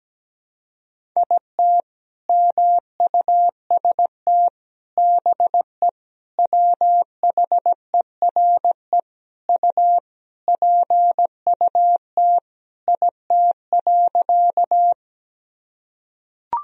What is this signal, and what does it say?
Morse code, heard as a keyed tone, about 17 wpm, 700 Hz. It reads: IT MUST BE WHERE U PUT IT.
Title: Sentences from Top 200 Words and Abbreviations 17wpm